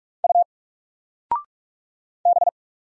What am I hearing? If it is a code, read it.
Morse code, heard as a keyed tone, about 45 wpm, 700 Hz. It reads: U B
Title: Single Letter-Number 45wpm